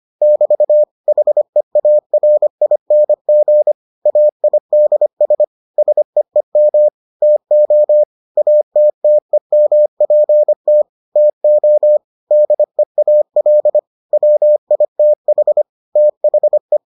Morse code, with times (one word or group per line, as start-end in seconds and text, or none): 0.2-0.8 <BT>
1.1-3.7 HEARING
4.1-5.4 AIDS
5.8-6.9 SEEM
7.2-8.0 TO
8.4-10.8 ATTEMPT
11.2-12.0 TO
12.3-13.8 DEAL
14.1-15.6 WITH
16.0-16.8 THE